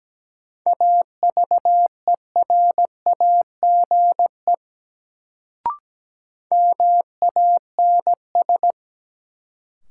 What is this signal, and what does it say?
Morse code, heard as a keyed tone, about 17 wpm, 700 Hz. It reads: AVERAGE MANS
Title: Sentences from Top 1000 Words - Review Part 6 - New Word Drill - 17wpm